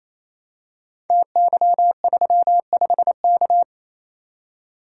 Morse code, heard as a keyed tone, about 29 wpm, 700 Hz. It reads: TY35K
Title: Code Groups - 5 Letter:Numbers - 28wpm